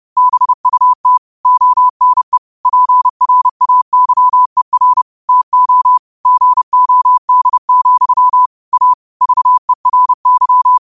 Morse code, heard as a keyed tone, about 30 wpm, 1000 Hz. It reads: BUT ONE PRAYER TO GOD, A VERY
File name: 30wpm